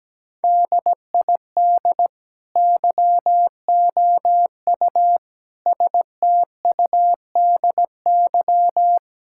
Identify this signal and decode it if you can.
Morse code, heard as a keyed tone, about 17 wpm, 700 Hz. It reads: DID YOU STUDY
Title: Sentences from Top 1000 Words - Review Part 5 - 17wpm